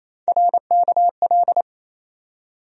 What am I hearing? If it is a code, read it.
Morse code, heard as a keyed tone, about 27 wpm, 700 Hz. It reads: RKL
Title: USA Call Sign Postfixes - 3 Letters 28wpm